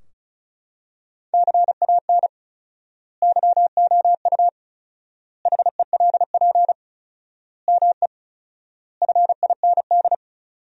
Morse code, heard as a keyed tone, about 36 wpm, 700 Hz. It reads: CAN YOU HELP ME FIND
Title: 4x Word-Spacing - Sentences from Top 200 Words - 35wpm